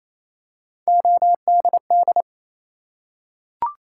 Morse code, heard as a keyed tone, about 28 wpm, 700 Hz. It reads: ODD
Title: Three Letter Words - 28wpm